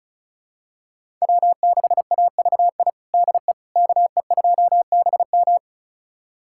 Morse code, heard as a keyed tone, about 36 wpm, 700 Hz. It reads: W6AVI DE KE2BM
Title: USA Callsign DE Callsign - 5 Characters - 35wpm